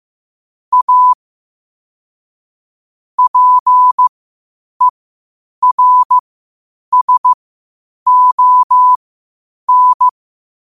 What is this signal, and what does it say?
Morse code, heard as a keyed tone, about 15 wpm, spaced as for 8 wpm, 1000 Hz. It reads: A PERSON
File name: qotdfiles / 8wpm